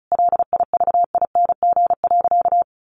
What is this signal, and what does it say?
Morse code, heard as a keyed tone, about 34 wpm, 700 Hz. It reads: LIVING.
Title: Sentences from Top 1500 Words 35wpm